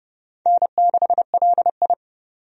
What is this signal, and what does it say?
Morse code, heard as a keyed tone, about 30 wpm, 700 Hz. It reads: N6LI
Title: USA Call Signs - 4 Characters - 30wpm